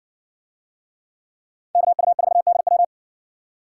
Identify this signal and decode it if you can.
Morse code, heard as a keyed tone, about 57 wpm, 700 Hz. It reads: KU4DW